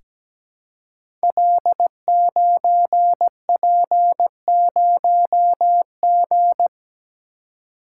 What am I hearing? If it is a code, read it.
Morse code, heard as a keyed tone, about 17 wpm, 700 Hz. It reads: L9P0G
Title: Code Groups - 5 Letter:Numbers - 17wpm